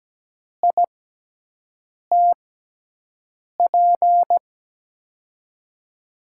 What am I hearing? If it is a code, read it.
Morse code, heard as a keyed tone, about 17 wpm, spaced as for 6 wpm, 700 Hz. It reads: ITP
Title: USA Call Signs - 6x Character Spacing - 17wpm